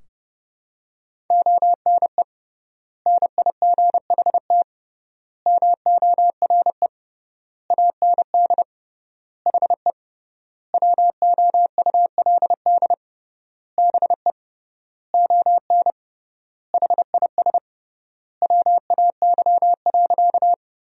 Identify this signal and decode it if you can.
Morse code, heard as a keyed tone, about 30 wpm, 700 Hz. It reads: ONE NIGHT MORE AND HE WOULD BE ON HIS WAY.
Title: 3x Word-Spacing - Sentences from Top 300 Words - 30wpm